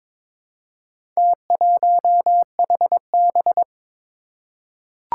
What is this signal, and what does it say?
Morse code, heard as a keyed tone, about 22 wpm, 700 Hz. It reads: T1HB
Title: Four Challenging Letter-Number Combinations 22wpm